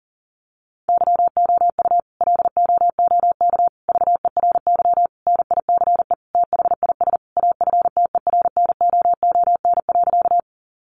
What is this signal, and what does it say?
Morse code, heard as a keyed tone, about 40 wpm, 700 Hz. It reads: YOU LOOK VERY NICE THIS AFTERNOON.
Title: Sentences from Top 1500 Words 40wpm